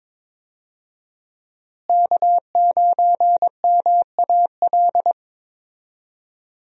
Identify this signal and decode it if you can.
Morse code, heard as a keyed tone, about 22 wpm, 700 Hz. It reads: K9MAL
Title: USA Call Signs - 5 Characters - Repeated 3x - 22wpm